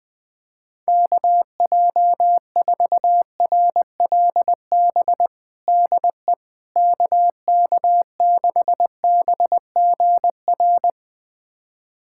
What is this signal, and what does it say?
Morse code, heard as a keyed tone, about 20 wpm, 700 Hz. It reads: KJ4RLB DE KK6BGR
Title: USA Callsign DE Callsign - 6 Characters - 20wpm